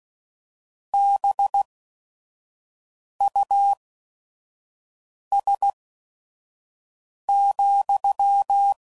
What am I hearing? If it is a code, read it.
Morse code, heard as a keyed tone, about 16 wpm, spaced as for 5 wpm, 800 Hz. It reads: BUS,